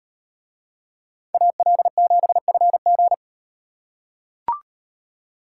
Morse code, heard as a keyed tone, about 36 wpm, 700 Hz. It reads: AL7FG